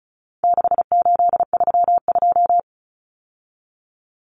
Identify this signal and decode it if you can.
Morse code, heard as a keyed tone, about 34 wpm, 700 Hz. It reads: 6832
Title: Numbers - 4-Digit 35wpm